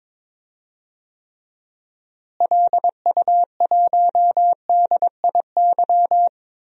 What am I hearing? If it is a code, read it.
Morse code, heard as a keyed tone, about 22 wpm, 700 Hz. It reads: LU1DIY
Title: DXCC-100 Call Signs - 6 Characters - Repeated 3x - 22wpm